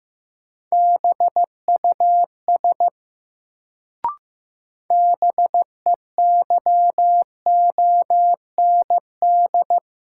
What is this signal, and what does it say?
Morse code, heard as a keyed tone, about 15 wpm, 700 Hz. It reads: BUS BEYOND
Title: Sentences from Top 1000 Words - Review Part 6 - New Word Drill - 15wpm